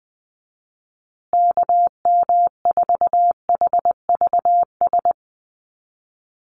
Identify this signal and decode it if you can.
Morse code, heard as a keyed tone, about 19 wpm, 700 Hz. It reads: KM4HVS
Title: USA Call Signs Repeated 2x 20wpm